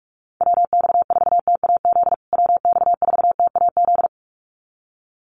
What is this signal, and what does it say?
Morse code, heard as a keyed tone, about 44 wpm, 700 Hz. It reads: WX4TAZ WX4TAZ
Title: USA Call Signs Repeated 2x 45wpm